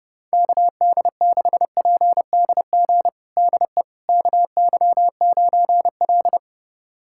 Morse code, heard as a keyed tone, about 32 wpm, 700 Hz. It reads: KD6PDG DE KY9L
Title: USA Callsign DE Callsign 30wpm